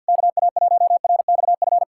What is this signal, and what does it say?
Morse code, heard as a keyed tone, about 52 wpm, 690 Hz. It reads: KA1RXF
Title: SOTA QSO Practice 50wpm